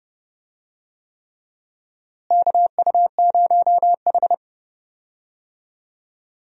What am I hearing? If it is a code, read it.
Morse code, heard as a keyed tone, about 30 wpm, 700 Hz. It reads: KU0H
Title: USA Call Signs - 4 Characters - Repeated 3x - 30wpm